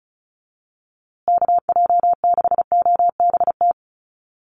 Morse code, heard as a keyed tone, about 34 wpm, 700 Hz. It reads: KJ6OBT
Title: USA Call Signs Repeated 3x 35wpm